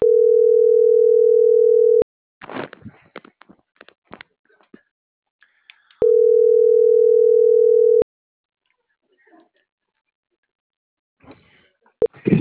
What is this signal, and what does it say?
Ringback tone (North America), 440 + 480 Hz.